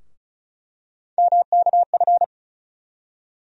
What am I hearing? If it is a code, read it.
Morse code, heard as a keyed tone, about 34 wpm, 700 Hz. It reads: MKF
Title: USA Call Sign Postfixes - 3 Letters 35wpm